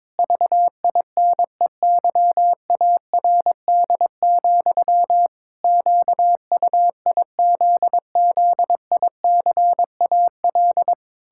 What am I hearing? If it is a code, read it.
Morse code, heard as a keyed tone, about 22 wpm, 700 Hz. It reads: VINEYARD, QUIZZICAL
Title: Pangram Challenge v2 - 22wpm